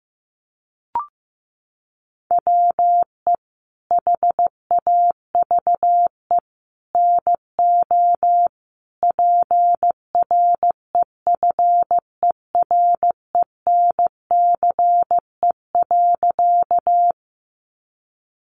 Morse code, heard as a keyed tone, about 15 wpm, 700 Hz. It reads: WE HAVE NO PREFERENCE.
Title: Sentences from Top 1500 Words 15wpm